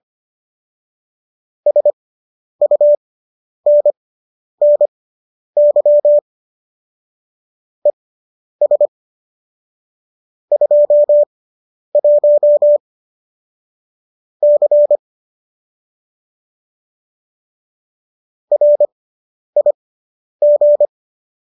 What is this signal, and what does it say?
Morse code, heard as a keyed tone, about 25 wpm, spaced as for 10 wpm, 590 Hz. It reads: SUNNY ES 21 C RIG